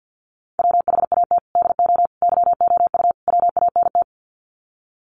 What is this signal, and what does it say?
Morse code, heard as a keyed tone, about 48 wpm, 700 Hz. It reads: WHAT DO YOU WANT
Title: Sets of 4 Words 50wpm